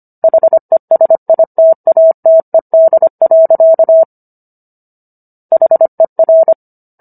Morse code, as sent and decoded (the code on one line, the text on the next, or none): .... ... .. - .- - . -.. .-.-.- / .... .-.
HESITATED. HER